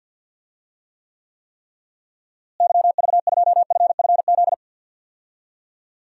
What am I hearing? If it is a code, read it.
Morse code, heard as a keyed tone, about 52 wpm, 700 Hz. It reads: YV2FFZ
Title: DXCC-100 Call Signs - 6 Characters - Repeated 3x - 50wpm